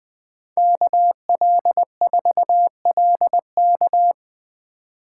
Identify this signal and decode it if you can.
Morse code, heard as a keyed tone, about 20 wpm, 700 Hz. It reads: KL4LK
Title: Speed Racing - USA Call Signs 17wpm